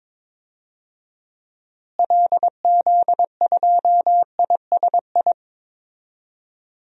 Morse code, heard as a keyed tone, about 22 wpm, 700 Hz. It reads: LZ2ISI
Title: DXCC-100 Call Signs - 6 Characters - Repeated 3x - 22wpm